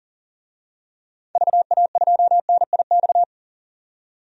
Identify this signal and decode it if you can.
Morse code, heard as a keyed tone, about 40 wpm, 700 Hz. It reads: VA2NIX
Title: DXCC-100 Call Signs - 6 Characters - 40wpm